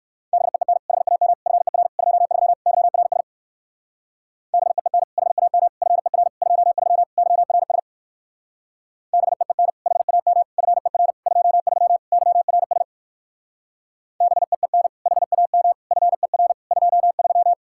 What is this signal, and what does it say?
Morse code, heard as a keyed tone, about 56 wpm, 700 Hz. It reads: BEEN HAM FER 23 YRS BEEN HAM FER 23 YRS BEEN HAM FER 23 YRS BEEN HAM FER 23
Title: Speed-Racing Common QSO Phrases - 45wpm